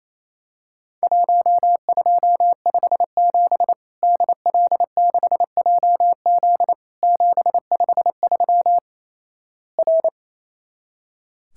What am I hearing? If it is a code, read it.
Morse code, heard as a keyed tone, about 28 wpm, 700 Hz. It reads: 1257 DL6JZ 753 R
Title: Worked All Europe DX Contest - QTC Exchange - 28wpm